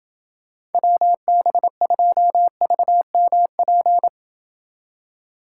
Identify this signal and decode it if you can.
Morse code, heard as a keyed tone, about 29 wpm, 700 Hz. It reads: WB2VMP